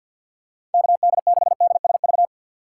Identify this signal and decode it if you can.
Morse code, heard as a keyed tone, about 53 wpm, 700 Hz. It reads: KD6DSV
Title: USA Call Signs - 6 Characters - 6x Character Spacing - 50wpm